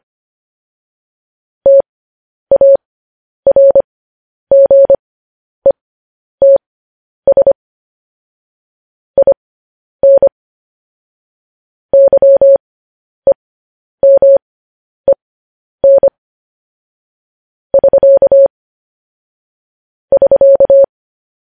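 Morse code, heard as a keyed tone, about 25 wpm, spaced as for 10 wpm, 550 Hz. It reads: TARGETS IN YEMEN <SK> <SK>
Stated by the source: Morse code news headlines transmission.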